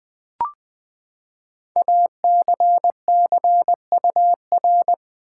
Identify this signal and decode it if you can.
Morse code, heard as a keyed tone, about 20 wpm, 700 Hz. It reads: ACCUR
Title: Build A Word - 5 Letters At A Time - 10-Letter Words - 20wpm